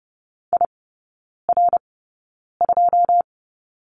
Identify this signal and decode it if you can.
Morse code, heard as a keyed tone, about 30 wpm, spaced as for 9 wpm, 700 Hz. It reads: IR2
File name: 081 Lesson 14e - Number 2 - Three Characters 30wpm